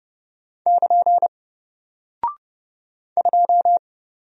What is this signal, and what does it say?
Morse code, heard as a keyed tone, about 30 wpm, 700 Hz. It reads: <KN> 2
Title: Single Character (Letter:Number:Punctuation:Prosign) 30wpm